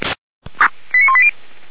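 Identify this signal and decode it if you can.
Five-tone selective call (ZVEI), heard as quick runs of five tones, tones of about 70 ms. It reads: 89180
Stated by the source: ZVEI five-tone call received on SDR.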